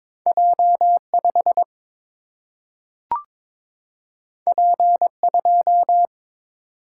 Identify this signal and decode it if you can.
Morse code, heard as a keyed tone, about 22 wpm, 700 Hz. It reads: J 5 P 2
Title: Single-Letter Single-Number 22wpm